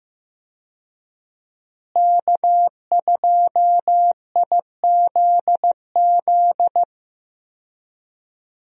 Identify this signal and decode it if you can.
Morse code, heard as a keyed tone, about 15 wpm, 700 Hz. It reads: K2IZZ
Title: USA Call Signs - 5 Characters - Repeated 2x - 15wpm